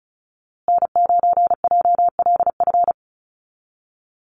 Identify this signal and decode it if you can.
Morse code, heard as a keyed tone, about 34 wpm, 700 Hz. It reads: N9JLF
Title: USA Call Signs Repeated 2x 35wpm